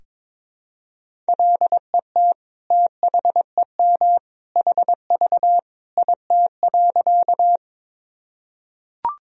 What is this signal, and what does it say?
Morse code, heard as a keyed tone, about 22 wpm, 700 Hz. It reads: LET THEM HV IT.
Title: Sentences from Top 300 Words and Abbreviations 22wpm